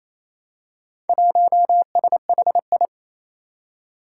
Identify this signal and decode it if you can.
Morse code, heard as a keyed tone, about 27 wpm, 700 Hz. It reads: 1SHI
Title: Four Challenging Letter-Number Combinations 28wpm